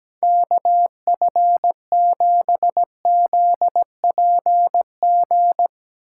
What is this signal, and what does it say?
Morse code, heard as a keyed tone, about 17 wpm, 700 Hz. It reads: KF7ZPG
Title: Speed Racing - USA 2x3 Call Signs 17wpm